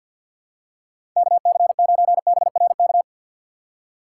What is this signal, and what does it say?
Morse code, heard as a keyed tone, about 48 wpm, 700 Hz. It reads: KC9BRK
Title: USA Call Signs - 6x Character Spacing - 50wpm